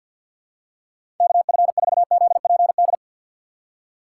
Morse code, heard as a keyed tone, about 53 wpm, 700 Hz. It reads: KF4ZPD
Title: USA Call Signs - 6 Characters - 8x Character Spacing - 50wpm